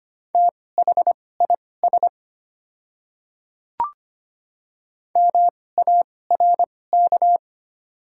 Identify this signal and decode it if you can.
Morse code, heard as a keyed tone, about 25 wpm, spaced as for 18 wpm, 700 Hz. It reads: THIS MARK
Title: 2x Character Spacing - Top 500 Words - 25wpm